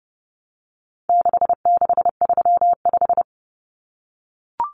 Morse code, heard as a keyed tone, about 30 wpm, 700 Hz. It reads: 6635
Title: Numbers - 4-Digit 30wpm